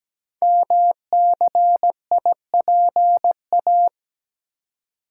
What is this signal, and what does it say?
Morse code, heard as a keyed tone, about 17 wpm, 700 Hz. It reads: MCIPA
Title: Code Groups - 5 Letters - 17wpm